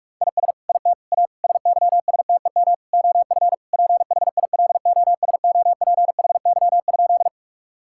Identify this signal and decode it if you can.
Morse code, heard as a keyed tone, about 41 wpm, 700 Hz. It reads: IS IT A SYSTEM OF PHILOSOPHY?
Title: Sentences from Top 1000 Words - Review Part 6 - 45wpm